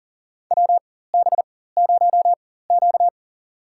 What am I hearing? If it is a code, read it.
Morse code, heard as a keyed tone, about 40 wpm, spaced as for 19 wpm, 700 Hz. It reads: WB0Q
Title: USA Call Signs - 4x Character Spacing - 40wpm